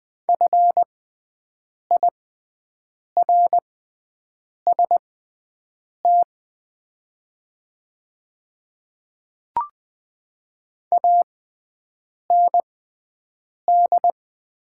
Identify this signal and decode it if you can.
Morse code, heard as a keyed tone, about 20 wpm, spaced as for 7 wpm, 690 Hz. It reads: FIRST AND
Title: 6x Character Spacing - Top 100 Words - 20wpm